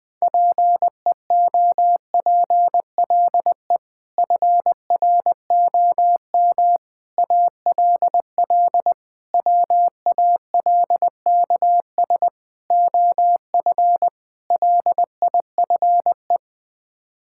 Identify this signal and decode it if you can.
Morse code, heard as a keyed tone, about 21 wpm, 700 Hz. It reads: PEOPLE FROM ALL WALKS OF LIFE
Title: Sets of 6 Words - 20wpm